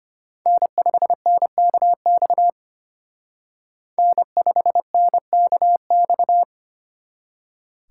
Morse code, heard as a keyed tone, about 33 wpm, 700 Hz. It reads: N5NKX N5NKX